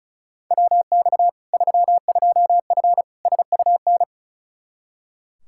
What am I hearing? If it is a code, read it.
Morse code, heard as a keyed tone, about 36 wpm, 700 Hz. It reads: WX 32F SUN
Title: Practice QSO - Exchange 2 - Info - 35wpm